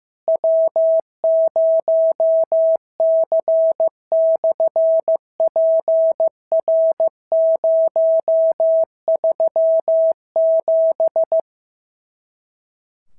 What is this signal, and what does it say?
Morse code, heard as a keyed tone, about 15 wpm, 650 Hz. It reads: W0C/PR037
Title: SOTA QSO Practice 15wpm